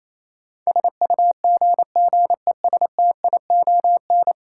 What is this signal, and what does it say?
Morse code, heard as a keyed tone, about 28 wpm, 700 Hz. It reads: SUGGESTION
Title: Sentences from Top 1600 Words - New Word Drill Alt - 28wpm